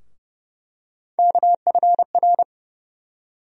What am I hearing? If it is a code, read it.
Morse code, heard as a keyed tone, about 32 wpm, 700 Hz. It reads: KFR
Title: USA Call Sign Postfixes - 3 Letters 30wpm